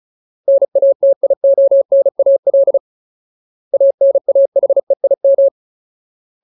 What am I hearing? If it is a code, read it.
Morse code, heard as a keyed tone, about 35 wpm, 550 Hz. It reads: NATIONAL ANAHEIM
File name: Ultimate Word List / 35WPM Ultimate